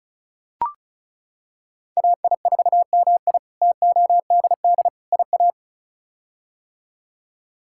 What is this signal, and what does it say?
Morse code, heard as a keyed tone, about 36 wpm, 700 Hz. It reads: AI4MI TODD IA
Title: K1USN SST QSO Practice - 35wpm